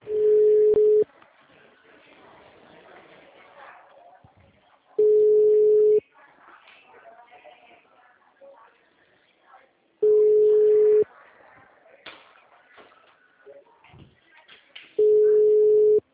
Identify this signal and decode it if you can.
Ringback tone (Europe, 425 Hz), with speech.
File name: out- -2003- - - .75093